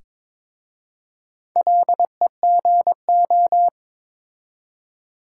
Morse code, heard as a keyed tone, about 22 wpm, 700 Hz. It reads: LEGO
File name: practice / 22wpm / Difficult Words with Hints - 22wpm